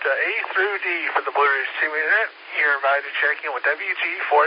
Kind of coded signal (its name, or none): Emergency Alert System (SAME) tones